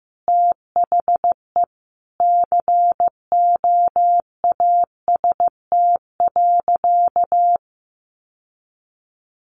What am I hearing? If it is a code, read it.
Morse code, heard as a keyed tone, about 15 wpm, 700 Hz. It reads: THE COAST.
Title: Sentences from Top 1300 Words 15wpm